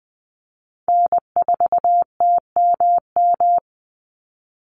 Morse code, heard as a keyed tone, about 19 wpm, 700 Hz. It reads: N4TMM
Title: Call Signs Repeated 3x - 20wpm